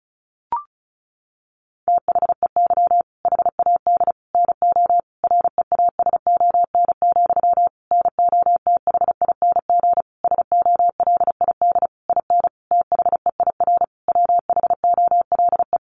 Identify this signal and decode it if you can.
Morse code, heard as a keyed tone, about 34 wpm, 700 Hz. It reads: THEY HAD NO REASON, NOTHING SOLID IN THEIR WHOLE
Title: Sentences from Top 1500 Words 35wpm